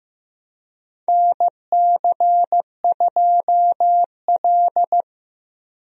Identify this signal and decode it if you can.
Morse code, heard as a keyed tone, about 15 wpm, 700 Hz. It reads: NC2L
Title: USA Call Signs - 4 Characters - 15wpm